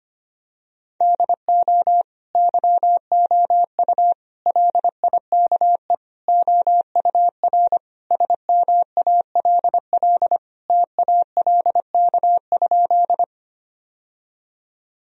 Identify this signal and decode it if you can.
Morse code, heard as a keyed tone, about 25 wpm, 700 Hz. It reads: DO YOU LIKE OUR SMALL TALK?
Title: Sentences from Top 1000 Words - Review Part 1 - 25wpm